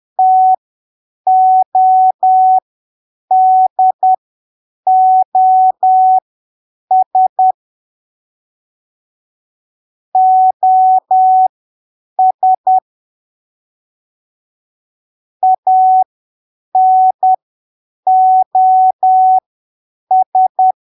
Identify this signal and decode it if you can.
Morse code, heard as a keyed tone, about 10 wpm, spaced as for 7 wpm, 750 Hz. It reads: TODOS OS ANOS
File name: arquivos / texto / texto 08 10ppm